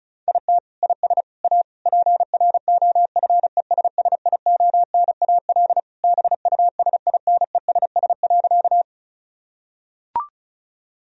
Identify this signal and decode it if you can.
Morse code, heard as a keyed tone, about 36 wpm, 700 Hz. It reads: IT IS A PROFESSIONAL BUSINESS.
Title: Sentences from Top 1000 Words - Review Part 4 - 35wpm